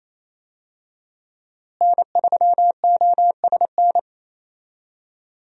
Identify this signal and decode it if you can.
Morse code, heard as a keyed tone, about 28 wpm, 700 Hz. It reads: N3OSN
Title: USA Call Signs - 5 Characters - Repeated 2x - 28wpm